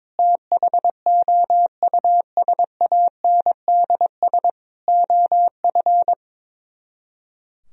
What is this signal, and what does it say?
Morse code, heard as a keyed tone, about 22 wpm, 700 Hz. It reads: THOUSANDS OF
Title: Speed-Racing - Sets of 4 Words 22wpm